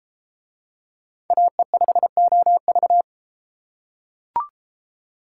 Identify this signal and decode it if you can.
Morse code, heard as a keyed tone, about 32 wpm, 700 Hz. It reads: AE5OV